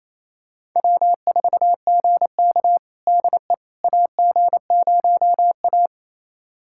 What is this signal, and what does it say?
Morse code, heard as a keyed tone, about 27 wpm, 700 Hz. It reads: W4GK DE AG0A
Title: USA Callsign DE Callsign - 4 Characters - 28wpm